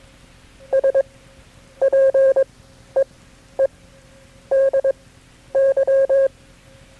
Morse code, heard as a keyed tone, about 22 wpm, spaced as for 10 wpm, 540 Hz. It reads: SPEEDY
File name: KB6MT-tape3-side1